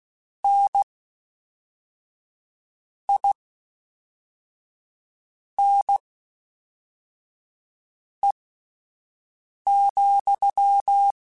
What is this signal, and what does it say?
Morse code, heard as a keyed tone, about 16 wpm, spaced as for 4 wpm, 800 Hz. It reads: NINE,
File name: Misc / Misc-mp3s / WordPairs sp @5